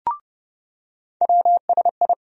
Morse code, heard as a keyed tone, about 30 wpm, 700 Hz. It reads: WSI
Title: USA Call Sign Postfixes - 3 Letters 30wpm